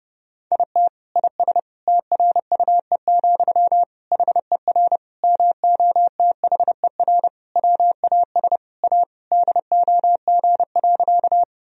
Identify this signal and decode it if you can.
Morse code, heard as a keyed tone, about 32 wpm, 700 Hz. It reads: IT IS TRUE, HER MOTHER WAS A DOG.
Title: Sentences from Top 1000 Words - Review Part 2 - 30wpm